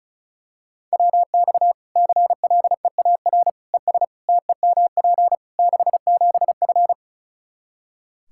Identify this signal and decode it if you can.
Morse code, heard as a keyed tone, about 34 wpm, 700 Hz. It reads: WX CLEAR ES TEMP 67F
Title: K1USN SST QSO Practice 35wpm